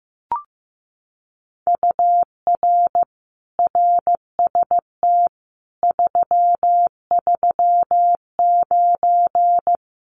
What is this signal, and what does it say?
Morse code, heard as a keyed tone, about 15 wpm, 700 Hz. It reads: UR RST 339
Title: Common QSO Phrases Encore 15wpm